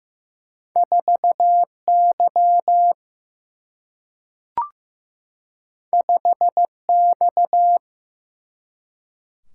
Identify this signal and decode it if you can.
Morse code, heard as a keyed tone, about 15 wpm, 700 Hz. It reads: 4 Y 5 X
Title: Single-Letter Single-Number 15wpm